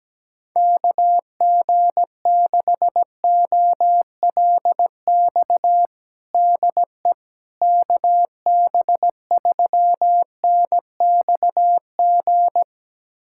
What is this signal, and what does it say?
Morse code, heard as a keyed tone, about 17 wpm, 700 Hz. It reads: KG6OLX DE KB3NXG
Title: USA Callsign DE Callsign - 6 Characters - 17wpm